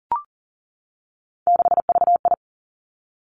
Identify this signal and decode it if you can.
Morse code, heard as a keyed tone, about 40 wpm, 700 Hz. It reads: 6VI
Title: Three Challenging Letter-Number Combinations 40wpm